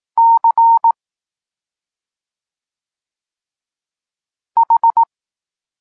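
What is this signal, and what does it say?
Morse code, heard as a keyed tone, about 18 wpm, 920 Hz. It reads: C H
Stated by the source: Polish Morse lesson teaching the letter C, from SP9MOA Niepołomicki Klub Krótkofalowców.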